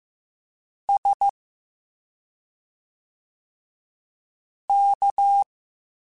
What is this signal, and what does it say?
Morse code, heard as a keyed tone, about 15 wpm, spaced as for 2 wpm, 800 Hz. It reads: SK